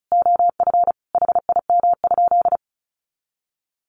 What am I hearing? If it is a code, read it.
Morse code, heard as a keyed tone, about 34 wpm, 700 Hz. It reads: OF HIM?
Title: Sentences from Top 100 Words 35wpm